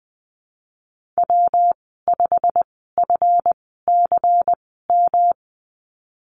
Morse code, heard as a keyed tone, about 20 wpm, spaced as for 15 wpm, 700 Hz. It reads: W5FCM
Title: 179 Lesson 29g - Slash - Call Signs 20wpm